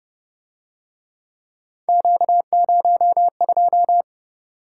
Morse code, heard as a keyed tone, about 32 wpm, 700 Hz. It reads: Q02